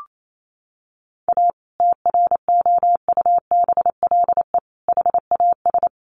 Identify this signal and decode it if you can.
Morse code, heard as a keyed tone, about 28 wpm, 700 Hz. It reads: A TROUBLE HAS
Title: Sentences from Top 1200 Words 28wpm